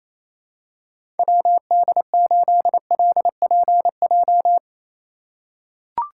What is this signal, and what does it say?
Morse code, heard as a keyed tone, about 28 wpm, 700 Hz. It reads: WD8LPJ
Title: USA Call Signs - 6 Characters - 2x Character Spacing - 28wpm